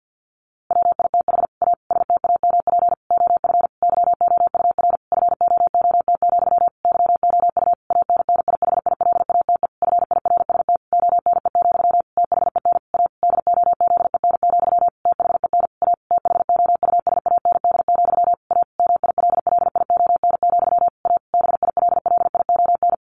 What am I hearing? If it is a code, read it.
Morse code, heard as a keyed tone, about 50 wpm, 700 Hz. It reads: WITH A STAMP OF YOUR FOOT, YOU ANNIHILATE FIRST ONE, THEN A DOZEN, THEN A THOUSAND, A MILLION, A BILLION
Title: A Sound Of Thunder 50wpm